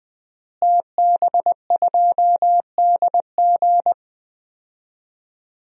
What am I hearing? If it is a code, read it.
Morse code, heard as a keyed tone, about 20 wpm, 700 Hz. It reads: TB2DG
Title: DXCC-100 Call Signs - 5 Characters - Repeated 2x - 20wpm